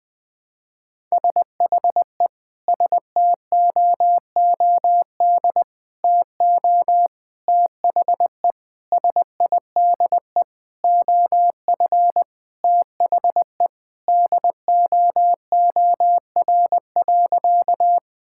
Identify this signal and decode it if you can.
Morse code, heard as a keyed tone, about 20 wpm, 700 Hz. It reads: SHE STOOD TO THE SIDE OF THE DOOR.
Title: Sentences with 5-Word Phrases - 20wpm